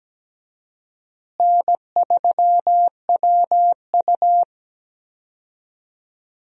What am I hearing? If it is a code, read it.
Morse code, heard as a keyed tone, about 17 wpm, 700 Hz. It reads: N3WU
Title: USA Call Signs - 4 Characters - Repeated 2x - 17wpm